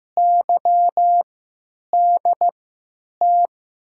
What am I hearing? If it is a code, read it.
Morse code, heard as a keyed tone, about 15 wpm, spaced as for 8 wpm, 700 Hz. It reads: YDT
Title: USA Call Signs - 6 Characters - 3x Character Spacing - 15wpm